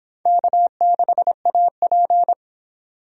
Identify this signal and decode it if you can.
Morse code, heard as a keyed tone, about 25 wpm, 700 Hz. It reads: K6AP